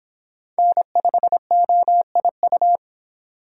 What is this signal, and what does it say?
Morse code, heard as a keyed tone, about 26 wpm, 700 Hz. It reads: N5OIU